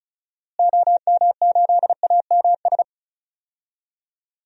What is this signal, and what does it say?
Morse code, heard as a keyed tone, about 36 wpm, 700 Hz. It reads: OM8AMS
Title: DXCC-100 Call Signs - 6 Characters - Repeated 3x - 35wpm